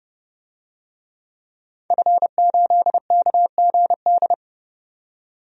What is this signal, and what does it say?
Morse code, heard as a keyed tone, about 30 wpm, 700 Hz. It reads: F8KGD
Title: DXCC-100 Call Signs - 5 Characters - 30wpm